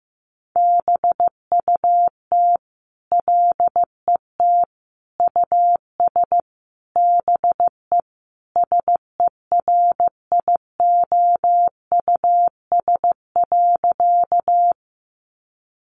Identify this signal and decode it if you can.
Morse code, heard as a keyed tone, about 15 wpm, 700 Hz. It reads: BUT LET US BE SERIOUS.
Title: Sentences from Top 1100 Words 15wpm